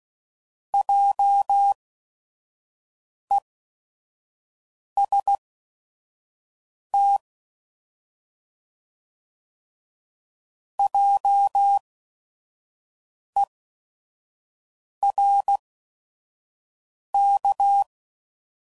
Morse code, heard as a keyed tone, about 16 wpm, spaced as for 5 wpm, 800 Hz. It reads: JEST JERK